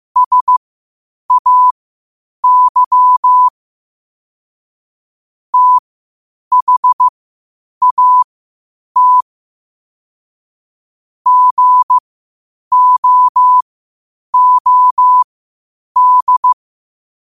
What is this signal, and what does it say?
Morse code, heard as a keyed tone, about 15 wpm, spaced as for 8 wpm, 1000 Hz. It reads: SAY THAT GOOD
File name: qotdfiles / 8wpm